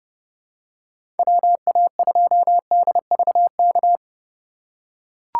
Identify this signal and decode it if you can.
Morse code, heard as a keyed tone, about 30 wpm, 700 Hz. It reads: WA2DVK
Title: USA Call Signs - 6 Characters - 4x Character Spacing - 30wpm